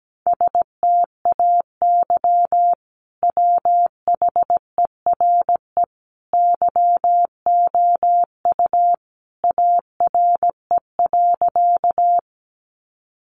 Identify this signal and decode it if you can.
Morse code, heard as a keyed tone, about 17 wpm, 700 Hz. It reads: STAY WHERE YOU ARE.
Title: Sentences from Top 500 Words 17wpm